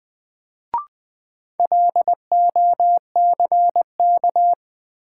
Morse code, heard as a keyed tone, about 20 wpm, 700 Hz. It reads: LOCK
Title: Sentences from Top 1300 Words - New Word Drill - 20wpm